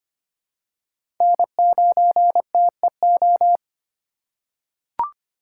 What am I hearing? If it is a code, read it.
Morse code, heard as a keyed tone, about 25 wpm, 700 Hz. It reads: N9TEO
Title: USA Call Signs - 5 Characters - 6x Character Spacing - 25wpm